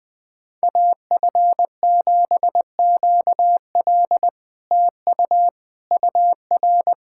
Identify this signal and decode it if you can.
Morse code, heard as a keyed tone, about 20 wpm, 700 Hz. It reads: AF7QL TU UR
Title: SOTA QSO Practice 20wpm